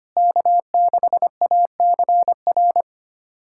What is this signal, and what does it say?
Morse code, heard as a keyed tone, about 25 wpm, 700 Hz. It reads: K6ACR
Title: USA Callsign DE Callsign - 5 Characters - 25wpm